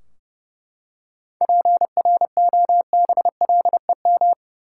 Morse code, heard as a keyed tone, about 30 wpm, 700 Hz. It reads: PROBLEM
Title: Seven Letter Words - 30wpm